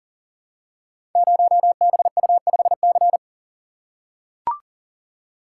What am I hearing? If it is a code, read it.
Morse code, heard as a keyed tone, about 41 wpm, 700 Hz. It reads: 0BU5C
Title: Code Groups - 5 Letter:Numbers - 40wpm